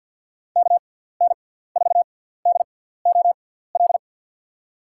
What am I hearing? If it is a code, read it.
Morse code, heard as a keyed tone, about 52 wpm, spaced as for 17 wpm, 700 Hz. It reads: KN4DOL
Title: USA Call Signs - 6x Character Spacing - 50wpm